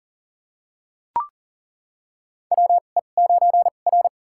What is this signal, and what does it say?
Morse code, heard as a keyed tone, about 40 wpm, spaced as for 29 wpm, 700 Hz. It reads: WE9R
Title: USA Call Signs - 4 Characters - 2x Character Spacing - 40wpm